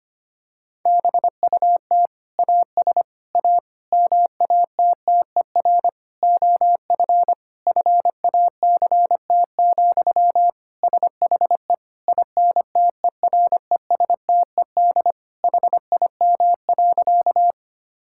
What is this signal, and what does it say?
Morse code, heard as a keyed tone, about 25 wpm, 700 Hz. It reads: BUT AS A MATTER OF FACT, SHE INTERESTED HIM.
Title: Sentences from Top 1000 Words - Review Part 6 - 25wpm